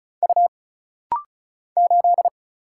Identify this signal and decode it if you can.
Morse code, heard as a keyed tone, about 36 wpm, 700 Hz. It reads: U 8
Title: Single Letter-Number - Rapid-Fire 35wpm